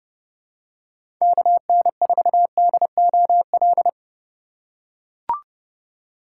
Morse code, heard as a keyed tone, about 31 wpm, 700 Hz. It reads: KN4DOL
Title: USA Call Signs - 4x Character Spacing - 30wpm